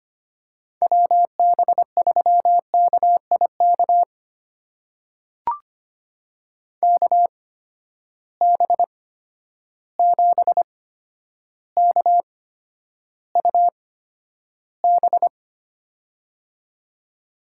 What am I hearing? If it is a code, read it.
Morse code, heard as a keyed tone, about 25 wpm, spaced as for 11 wpm, 700 Hz. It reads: WB3KIK KB7KUB
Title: USA Call Signs - 6 Characters - 8x Character Spacing - 25wpm